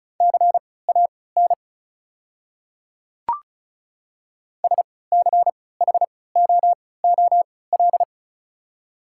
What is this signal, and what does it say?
Morse code, heard as a keyed tone, about 34 wpm, spaced as for 20 wpm, 690 Hz. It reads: CAN SCHOOL
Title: 3x Character Spacing - Top 500 Words - 35wpm